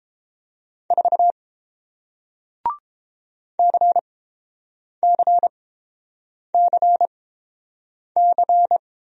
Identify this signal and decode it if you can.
Morse code, heard as a keyed tone, about 30 wpm, spaced as for 7 wpm, 700 Hz. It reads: 4 CCCC